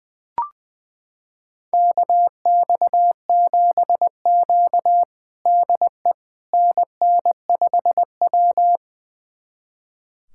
Morse code, heard as a keyed tone, about 20 wpm, 700 Hz. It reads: KX7Q DE NN5W
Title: USA Callsign DE Callsign - 4 Characters - 20wpm